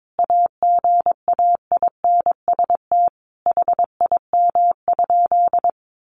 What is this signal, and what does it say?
Morse code, heard as a keyed tone, about 22 wpm, 700 Hz. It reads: AGAINST HIM?
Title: Sentences from Top 1000 Words 22wpm